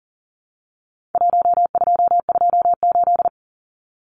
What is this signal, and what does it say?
Morse code, heard as a keyed tone, about 40 wpm, 700 Hz. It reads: 1228
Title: Numbers - Contest Style - 4-Digit 40wpm